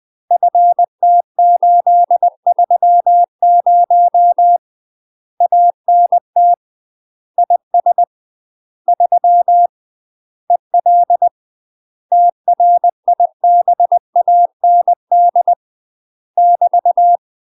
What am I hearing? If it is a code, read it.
Morse code, heard as a keyed tone, about 20 wpm, 700 Hz. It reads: FT830 ANT IS 3 EL TRIBAND <BT>
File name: UK MASTER QSOs TRAINING / 20WPM / UK MASTER QSOs 0015